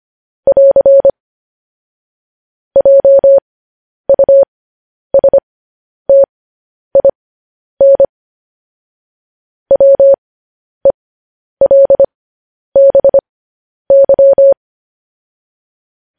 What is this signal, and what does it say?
Morse code, heard as a keyed tone, about 25 wpm, spaced as for 10 wpm, 550 Hz. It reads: <AR> JUSTIN WELBY
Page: news-headlines-in-morse-code-at-10-wpm